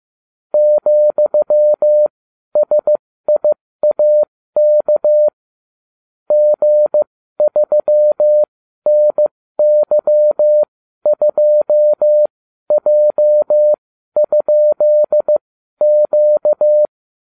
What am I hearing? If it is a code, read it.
Morse code, heard as a keyed tone, about 15 wpm, spaced as for 12 wpm, 600 Hz. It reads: ,SIAK G3NY2J?Q